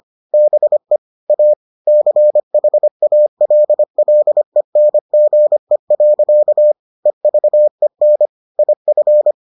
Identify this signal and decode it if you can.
Morse code, heard as a keyed tone, about 25 wpm, 600 Hz. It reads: BE A CHALLENGE. EVEN IF